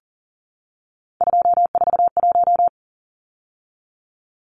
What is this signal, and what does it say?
Morse code, heard as a keyed tone, about 40 wpm, 700 Hz. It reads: 241
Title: Numbers - 3-Digit 40wpm